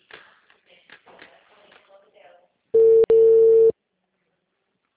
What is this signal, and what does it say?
Ringback tone (China, 450 Hz), with speech.
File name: out- -2033- - - .26162